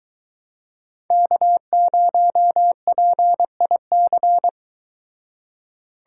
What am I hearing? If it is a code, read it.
Morse code, heard as a keyed tone, about 23 wpm, 700 Hz. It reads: K0PIC